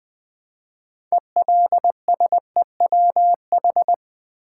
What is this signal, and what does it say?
Morse code, heard as a keyed tone, about 20 wpm, 700 Hz. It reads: ELSEWH
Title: Build A Word - 3 Letters At A Time - 9-Letter Words - 20wpm